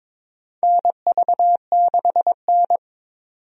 Morse code, heard as a keyed tone, about 22 wpm, 700 Hz. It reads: NV6N
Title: USA Call Signs - 4 Characters - 22wpm